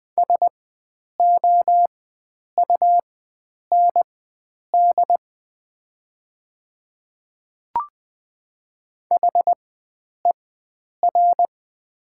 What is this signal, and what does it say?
Morse code, heard as a keyed tone, about 20 wpm, spaced as for 9 wpm, 690 Hz. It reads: SOUND HER
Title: 4x Character Spacing - Top 100 Words - 20wpm